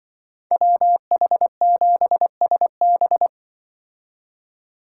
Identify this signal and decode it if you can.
Morse code, heard as a keyed tone, about 23 wpm, 700 Hz. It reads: WH7SB
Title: Speed Racing - USA 2x2 Call Signs 20wpm